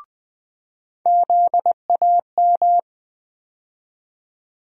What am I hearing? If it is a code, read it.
Morse code, heard as a keyed tone, about 21 wpm, 700 Hz. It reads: ZAM
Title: USA Call Sign Postfixes - 3 Letters 20wpm